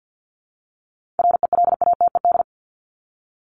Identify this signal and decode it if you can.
Morse code, heard as a keyed tone, about 50 wpm, 700 Hz. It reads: RELATED
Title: Top 751-1000 Words 50wpm